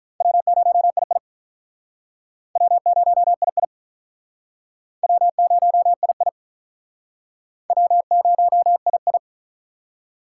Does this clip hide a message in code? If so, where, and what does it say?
Yes: Morse code from 0.2 to 9.2 s, heard as a keyed tone, about 44 wpm, 700 Hz. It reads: W0II W0II W0II W0II